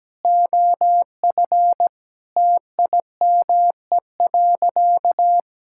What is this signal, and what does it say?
Morse code, heard as a keyed tone, about 17 wpm, 700 Hz. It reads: OF TIME.
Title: Sentences from Top 1000 Words - Review Part 5 - 17wpm